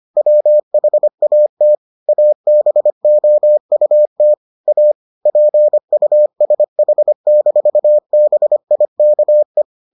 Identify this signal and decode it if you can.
Morse code, heard as a keyed tone, about 25 wpm, 600 Hz. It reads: WHAT ABOUT A PUSH-BIKE